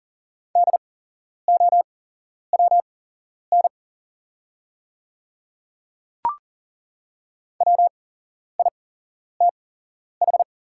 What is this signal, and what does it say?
Morse code, heard as a keyed tone, about 40 wpm, spaced as for 11 wpm, 690 Hz. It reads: DOWN WITH
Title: 8x Character Spacing - Top 300 Words - 40wpm